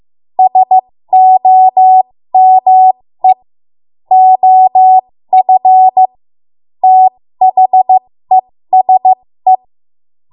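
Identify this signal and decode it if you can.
Morse code, heard as a keyed tone, about 15 wpm, spaced as for 13 wpm, 750 Hz. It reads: SOME OF THESE
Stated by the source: 13 wpm Morse code practice archive transmission.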